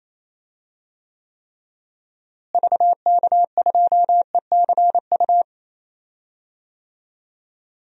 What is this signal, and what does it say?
Morse code, heard as a keyed tone, about 28 wpm, 700 Hz. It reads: VK2ECU
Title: DXCC-100 Call Signs - 6 Characters - Repeated 3x - 28wpm